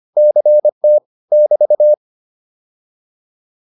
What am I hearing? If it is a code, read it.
Morse code, heard as a keyed tone, about 25 wpm, 600 Hz. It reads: CT <BT>